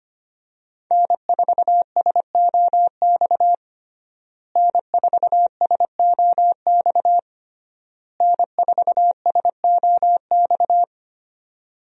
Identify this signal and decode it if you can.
Morse code, heard as a keyed tone, about 25 wpm, 700 Hz. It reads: N4SOX N4SOX N4SOX
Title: USA Call Signs - 5 Characters - Repeated 3x - 25wpm